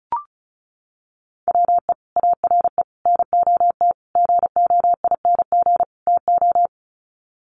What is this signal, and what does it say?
Morse code, heard as a keyed tone, about 34 wpm, 700 Hz. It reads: WE ARE NOT GOING TO
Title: Sets of 5 Words 35wpm